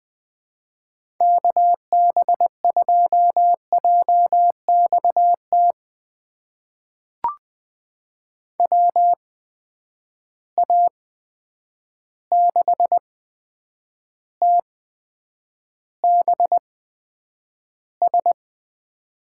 Morse code, heard as a keyed tone, about 21 wpm, spaced as for 9 wpm, 700 Hz. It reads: KB2JXT WA6TBS
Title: USA Call Signs - 6 Characters - 8x Character Spacing - 20wpm